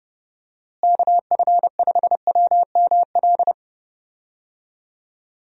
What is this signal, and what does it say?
Morse code, heard as a keyed tone, about 31 wpm, 700 Hz. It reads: KF5WML
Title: USA Call Signs - 6 Characters - Repeated 3x - 30wpm